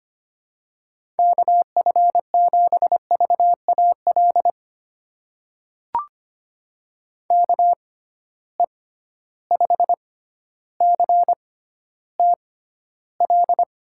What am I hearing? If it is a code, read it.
Morse code, heard as a keyed tone, about 25 wpm, spaced as for 13 wpm, 700 Hz. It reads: KF7VAL KE5CTL
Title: USA Call Signs - 6 Characters - 6x Character Spacing - 25wpm